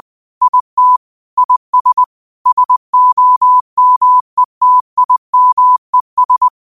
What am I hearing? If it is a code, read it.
Morse code, heard as a keyed tone, about 20 wpm, 1000 Hz. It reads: IT IS SOMETIMES
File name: 20wpm